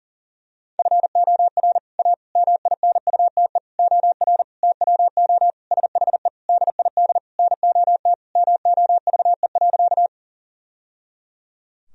Morse code, heard as a keyed tone, about 40 wpm, 700 Hz. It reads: FOR A MINUTE OR TWO SHE DID NOT MOVE.
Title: Sentences from Top 1000 Words - Review Part 1 - 40wpm